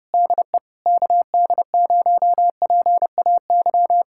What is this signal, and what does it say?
Morse code, heard as a keyed tone, about 32 wpm, 700 Hz. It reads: DE KD0PAY